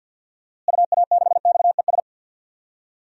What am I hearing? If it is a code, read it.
Morse code, heard as a keyed tone, about 51 wpm, 700 Hz. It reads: UA6XES
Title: DXCC-100 Call Signs - 6 Characters - Repeated 2x - 50wpm